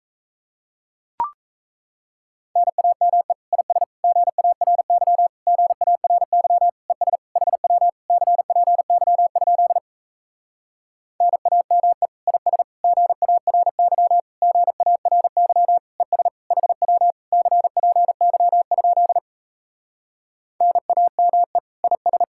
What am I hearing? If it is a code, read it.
Morse code, heard as a keyed tone, about 38 wpm, 700 Hz. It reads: NAME IS GARY GARY ES HW CPY? NAME IS GARY GARY ES HW CPY? NAME IS